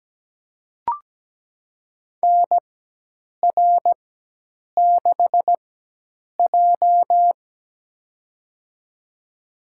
Morse code, heard as a keyed tone, about 17 wpm, spaced as for 8 wpm, 700 Hz. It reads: NR6J
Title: USA Call Signs - 4 Characters - 4x Character Spacing - 17wpm